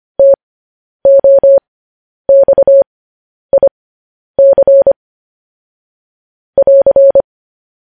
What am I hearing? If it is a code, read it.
Morse code, heard as a keyed tone, about 25 wpm, spaced as for 10 wpm, 550 Hz. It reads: TOXIC <AR>
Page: news-headlines-in-morse-code-at-10-wpm